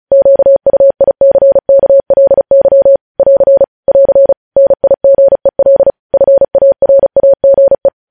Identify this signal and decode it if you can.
Morse code, heard as a keyed tone, about 36 wpm, 550 Hz. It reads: QUICKLY <AR> <AR> NIGEL FARAGE
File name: podcast / News headlines-1x-35x35